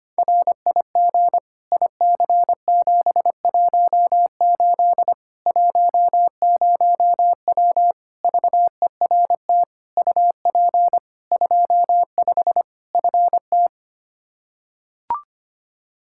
Morse code, heard as a keyed tone, about 25 wpm, 700 Hz. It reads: RIG IC718 10W VERT UP 25 FT
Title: Practice QSO - Exchange 2 - Info - 25wpm